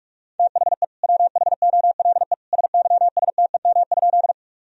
Morse code, heard as a keyed tone, about 46 wpm, 700 Hz. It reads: THE WHOLE SYSTEM?
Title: Sentences from Top 1000 Words - Review Part 3 - 45wpm